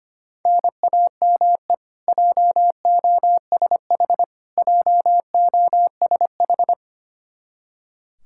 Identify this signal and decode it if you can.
Morse code, heard as a keyed tone, about 25 wpm, 700 Hz. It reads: NAME JOSH JOSH
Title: K1USN SST QSO Practice 25wpm